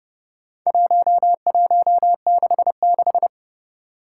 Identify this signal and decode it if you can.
Morse code, heard as a keyed tone, about 30 wpm, 700 Hz. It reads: 1166
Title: Four Challenging Letter-Number Combinations 30wpm